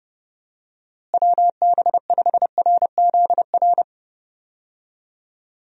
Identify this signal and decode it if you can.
Morse code, heard as a keyed tone, about 30 wpm, 700 Hz. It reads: WB5RZR
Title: USA Call Signs - 6 Characters - Repeated 3x - 30wpm